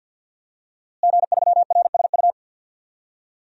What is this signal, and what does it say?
Morse code, heard as a keyed tone, about 51 wpm, 700 Hz. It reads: G3RSU
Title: DXCC-100 Call Signs - 5 Characters - Repeated 2x - 50wpm